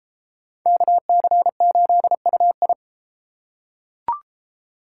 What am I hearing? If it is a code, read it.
Morse code, heard as a keyed tone, about 33 wpm, 700 Hz. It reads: KC8UI